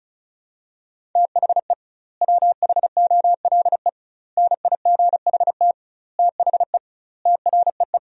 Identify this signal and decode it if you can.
Morse code, heard as a keyed tone, about 36 wpm, 700 Hz. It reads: THE WHOLE NIGHT THE TREE
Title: 2x Word-Spacing - Sentences from Top 500 Words - 35wpm